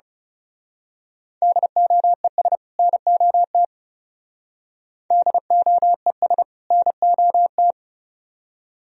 Morse code, heard as a keyed tone, about 34 wpm, 700 Hz. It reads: DOES NOT DOES NOT
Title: Speed-Racing - Sets of 2 Words 30wpm